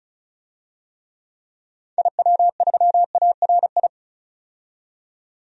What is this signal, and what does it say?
Morse code, heard as a keyed tone, about 36 wpm, 700 Hz. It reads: IW3ARI
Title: DXCC-100 Call Signs - 6 Characters - 35wpm